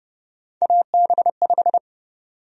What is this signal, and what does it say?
Morse code, heard as a keyed tone, about 30 wpm, 700 Hz. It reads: AB5
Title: USA Call Sign Prefixes - 30wpm